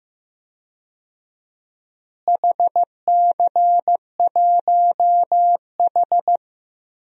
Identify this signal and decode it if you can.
Morse code, heard as a keyed tone, about 15 wpm, 700 Hz. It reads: HC1H
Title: DXCC-100 Call Signs - 4 Characters - Repeated 2x - 15wpm